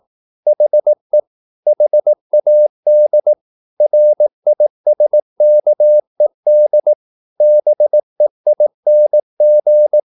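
Morse code, heard as a keyed tone, about 18 wpm, 600 Hz. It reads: HE HAD RISKED BEING
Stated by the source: Morse practice at 18 wpm, English text, from CW med Gnister.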